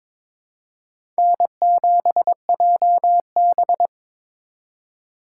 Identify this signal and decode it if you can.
Morse code, heard as a keyed tone, about 21 wpm, 700 Hz. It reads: N7JB
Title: USA Call Signs - 4 Characters - 3x Character Spacing - 22wpm